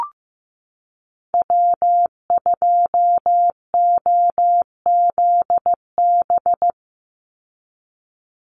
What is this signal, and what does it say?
Morse code, heard as a keyed tone, about 15 wpm, 700 Hz. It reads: W2OZB
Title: Call Signs Repeated 2x - 15wpm